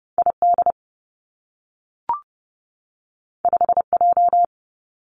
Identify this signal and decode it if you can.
Morse code, heard as a keyed tone, about 30 wpm, 700 Hz. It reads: I D 5 J
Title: Two Challenging Letter-Number Combinations 30wpm